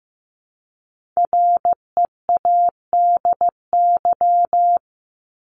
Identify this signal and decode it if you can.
Morse code, heard as a keyed tone, about 15 wpm, 700 Hz. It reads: READY
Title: Top 500 Words 15wpm